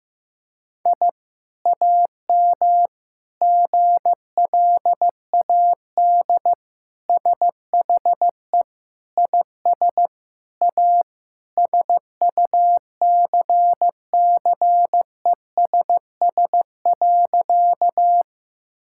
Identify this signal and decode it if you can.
Morse code, heard as a keyed tone, about 15 wpm, 700 Hz. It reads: I AM GLAD SHE IS A SUCCESS.
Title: Sentences from Top 1000 Words - Review Part 6 - 15wpm